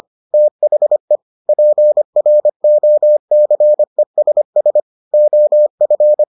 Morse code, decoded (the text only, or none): THE PROCESS OF